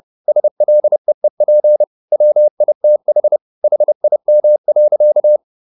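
Morse code, heard as a keyed tone, about 30 wpm, 600 Hz. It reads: SLEEP WITH HIM.
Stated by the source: Morse code at 30 wpm, the novel Never by Ken Follett in English, from CW med Gnister.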